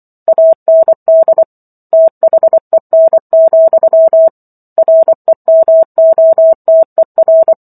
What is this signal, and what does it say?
Morse code, heard as a keyed tone, about 24 wpm, 650 Hz. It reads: AND THEN, REMOTER
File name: Book One / 15-Chapter 15 - 24 WPM